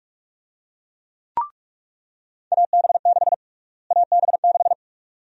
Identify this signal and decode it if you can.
Morse code, heard as a keyed tone, about 43 wpm, 700 Hz. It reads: AB6 AB6
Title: USA Call Sign Prefixes - Repeated 2x - 45wpm